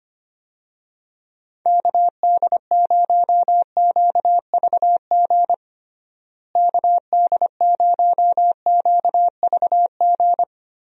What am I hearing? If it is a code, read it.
Morse code, heard as a keyed tone, about 25 wpm, 700 Hz. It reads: KD0QVG KD0QVG
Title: USA Call Signs - 6 Characters - Repeated 3x - 25wpm